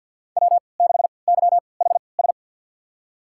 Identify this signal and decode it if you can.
Morse code, heard as a keyed tone, about 47 wpm, spaced as for 28 wpm, 700 Hz. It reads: W6YHS
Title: USA Call Signs - 5 Characters - 3x Character Spacing - 50wpm